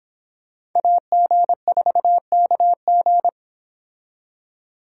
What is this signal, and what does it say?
Morse code, heard as a keyed tone, about 26 wpm, 700 Hz. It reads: AG4KG